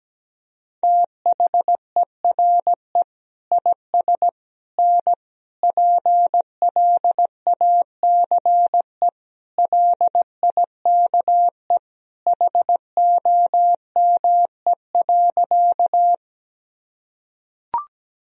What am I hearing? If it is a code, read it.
Morse code, heard as a keyed tone, about 17 wpm, 700 Hz. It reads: THERE IS N PLACE LIKE HOME.
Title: Sentences from Top 200 Words and Abbreviations 17wpm